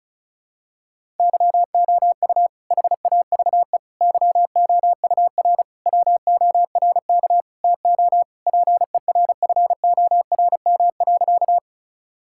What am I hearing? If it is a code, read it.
Morse code, heard as a keyed tone, about 36 wpm, 700 Hz. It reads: YOU HAVE YOUR WORK TO PERFORM.
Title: Sentences from Top 1000 Words - Review Part 5 - 35wpm